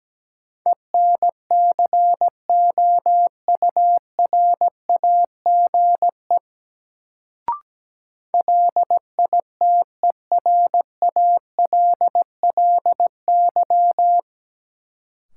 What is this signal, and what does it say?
Morse code, heard as a keyed tone, about 17 wpm, 700 Hz. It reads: ENCOURAGE LITERALLY
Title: Sentences from Top 1400 Words - New Word Drill Alt - 17wpm